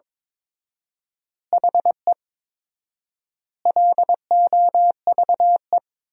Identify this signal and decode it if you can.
Morse code, heard as a keyed tone, about 22 wpm, 700 Hz. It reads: HE LOVE
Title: 4x Word-Spacing - Sentences from Top 400 Words - 22wpm